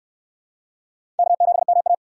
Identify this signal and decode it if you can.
Morse code, heard as a keyed tone, about 62 wpm, 700 Hz. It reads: K7MA